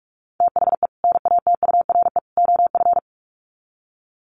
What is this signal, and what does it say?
Morse code, heard as a keyed tone, about 44 wpm, 700 Hz. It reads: THE NATURE OF
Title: Sets of 3 Words 45wpm